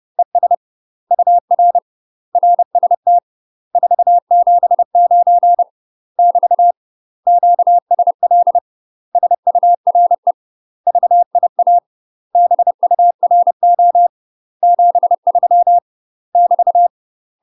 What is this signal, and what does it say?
Morse code, heard as a keyed tone, about 30 wpm, 700 Hz. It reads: ES UR RST 479 <BT> QSL SURE VIA BURO 73 <BT>
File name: UK MASTER QSOs TRAINING / 30WPM / UK MASTER QSOs 0023